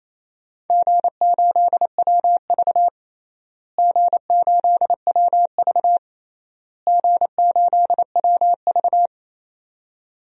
Morse code, heard as a keyed tone, about 27 wpm, 700 Hz. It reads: G8WV G8WV G8WV
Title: DXCC-100 Call Signs - 4 Characters - Repeated 3x - 28wpm